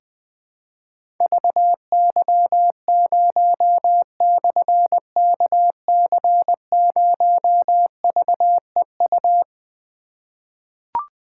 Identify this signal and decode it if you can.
Morse code, heard as a keyed tone, about 21 wpm, 700 Hz. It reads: VY0/KC0VEU
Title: Difficult Call Signs 20wpm